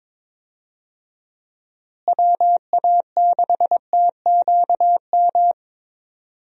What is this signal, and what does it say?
Morse code, heard as a keyed tone, about 22 wpm, 700 Hz. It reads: WA6TQM
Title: USA Call Signs - 6 Characters - Repeated 2x - 22wpm